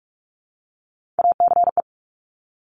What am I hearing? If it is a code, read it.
Morse code, heard as a keyed tone, about 46 wpm, 690 Hz. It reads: ACE
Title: Top Three Letter Combinations in Words 45wpm